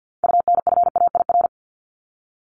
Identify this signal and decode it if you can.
Morse code, heard as a keyed tone, about 48 wpm, 700 Hz. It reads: UNFAIR
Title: Top 1751-2000 Words 50wpm